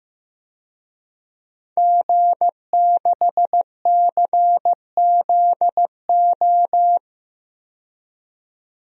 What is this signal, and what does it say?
Morse code, heard as a keyed tone, about 15 wpm, 700 Hz. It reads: G6CZO
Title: DXCC-100 Call Signs - 5 Characters - 15wpm